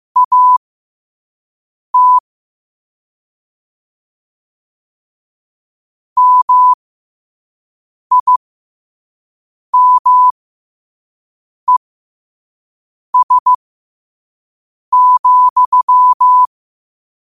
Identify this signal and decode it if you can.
Morse code, heard as a keyed tone, about 15 wpm, spaced as for 5 wpm, 1000 Hz. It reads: AT MIMES,
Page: New quotes every day in morse code at 5 Words per minute.